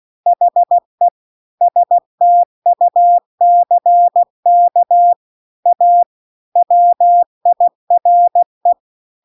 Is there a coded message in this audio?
Yes: Morse code from 0.3 to 8.7 s, heard as a keyed tone, about 16 wpm, 700 Hz. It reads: HE STUCK A WIRE